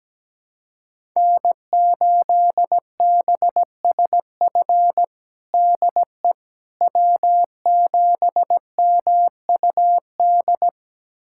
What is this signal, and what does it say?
Morse code, heard as a keyed tone, about 17 wpm, 700 Hz. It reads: N8BSF DE W7MUD
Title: USA Callsign DE Callsign - 5 Characters - 17wpm